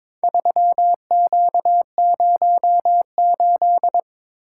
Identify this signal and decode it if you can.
Morse code, heard as a keyed tone, about 22 wpm, 700 Hz. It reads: 3Q08
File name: practice / 22wpm / Four Letter-Number Combinations - 22wpm